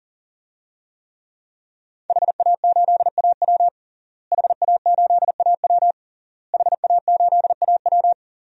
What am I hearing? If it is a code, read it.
Morse code, heard as a keyed tone, about 40 wpm, 700 Hz. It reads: HA8AW HA8AW HA8AW
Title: DXCC-100 Call Signs - 5 Characters - Repeated 3x - 40wpm